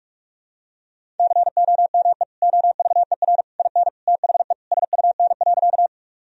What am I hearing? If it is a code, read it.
Morse code, heard as a keyed tone, about 41 wpm, 700 Hz. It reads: COME OVER IN THE SUN.
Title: Sentences from Top 1000 Words - Review Part 1 - 45wpm